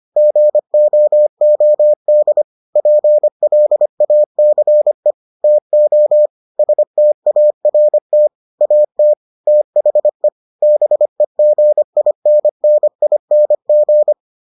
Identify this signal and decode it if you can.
Morse code, heard as a keyed tone, about 25 wpm, 600 Hz. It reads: GOOD PLACE TO START AT THE BEGINNING